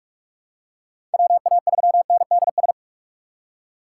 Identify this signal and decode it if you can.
Morse code, heard as a keyed tone, about 46 wpm, 700 Hz. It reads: WA3NDS
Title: USA Call Signs - 6 Characters - 3x Character Spacing - 45wpm